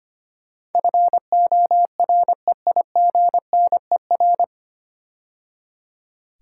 Morse code, heard as a keyed tone, about 25 wpm, 700 Hz. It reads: FOREIGNER
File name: practice / 25wpm / 9 Letter Words - First 3 Letters Given - 25wpm